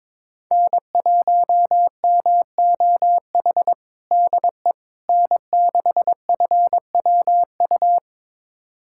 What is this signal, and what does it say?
Morse code, heard as a keyed tone, about 22 wpm, 700 Hz. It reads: N1MOH DE N6FWU
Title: USA Callsign DE Callsign - 5 Characters - 22wpm